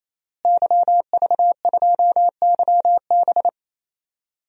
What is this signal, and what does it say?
Morse code, heard as a keyed tone, about 28 wpm, 700 Hz. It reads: YV2YB
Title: DXCC-100 Call Signs - 5 Characters - Repeated 3x - 28wpm